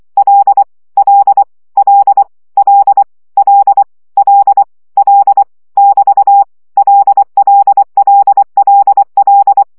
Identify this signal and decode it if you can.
Morse code, heard as a keyed tone, about 24 wpm, 800 Hz. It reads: L L L L L L L <BT> LLLLL
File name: CW gyakorlas l 20 baud